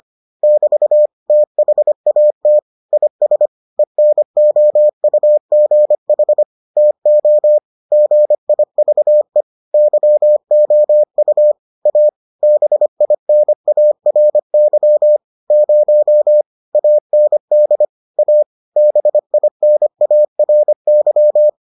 Morse code, heard as a keyed tone, about 25 wpm, 600 Hz. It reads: <BT> THAT IS ENOUGH TO GIVE YOU A BINARY 0 AND A BINARY